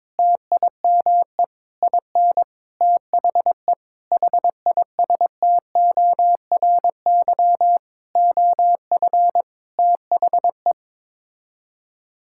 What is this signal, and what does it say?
Morse code, heard as a keyed tone, about 22 wpm, 700 Hz. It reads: TIME IN THE HISTORY OF THE
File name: Sets of 9 Words - 22wpm